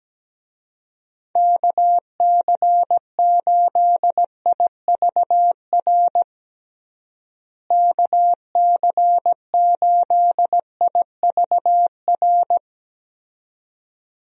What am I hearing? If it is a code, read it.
Morse code, heard as a keyed tone, about 17 wpm, 700 Hz. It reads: KC8IVR KC8IVR
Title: USA Call Signs - 6 Characters - Repeated 2x - 17wpm